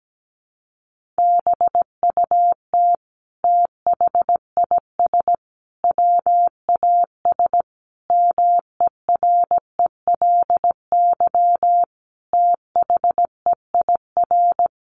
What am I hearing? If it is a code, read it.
Morse code, heard as a keyed tone, about 17 wpm, 700 Hz. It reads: BUT THIS WAS MERELY THEIR
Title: Sentences from Top 1400 Words 17wpm